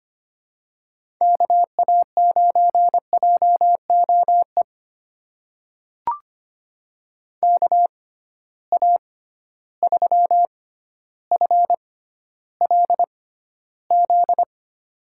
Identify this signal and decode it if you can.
Morse code, heard as a keyed tone, about 25 wpm, spaced as for 13 wpm, 700 Hz. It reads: KA9JOE KA3FLZ
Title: USA Call Signs - 6 Characters - 6x Character Spacing - 25wpm